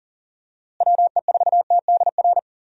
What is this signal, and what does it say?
Morse code, heard as a keyed tone, about 40 wpm, 700 Hz. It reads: WE4TDR
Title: USA Call Signs - 6 Characters - Repeated 3x - 40wpm